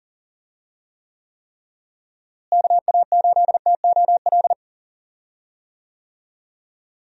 Morse code, heard as a keyed tone, about 40 wpm, 700 Hz. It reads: KA8TOL
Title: USA Call Signs - 6 Characters - Repeated 3x - 40wpm